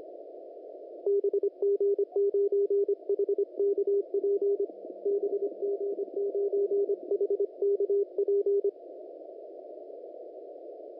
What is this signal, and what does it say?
Morse code, heard as a keyed tone, about 25 wpm, 400 Hz, with radio noise. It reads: BG9HKP BG9HKP
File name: chinese / FT991report / 17-44-43 Channel 1 clip